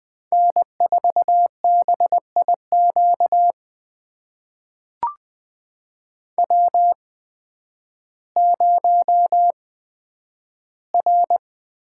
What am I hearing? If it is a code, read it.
Morse code, heard as a keyed tone, about 20 wpm, spaced as for 11 wpm, 700 Hz. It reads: N4BIQ W0R
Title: USA Call Signs - 8x Character Spacing - 20wpm